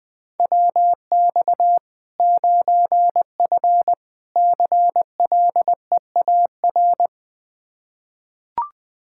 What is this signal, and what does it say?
Morse code, heard as a keyed tone, about 20 wpm, 700 Hz. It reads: WX 9F CLEAR
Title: K1USN SST QSO Practice 20wpm